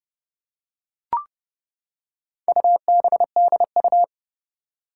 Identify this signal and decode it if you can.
Morse code, heard as a keyed tone, about 29 wpm, 700 Hz. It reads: UBDU
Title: Four Challenging Letter-Number Combinations 30wpm